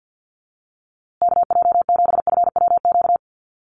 Morse code, heard as a keyed tone, about 53 wpm, 700 Hz. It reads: KJ7FWQ
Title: USA Call Signs Repeated 2x 50wpm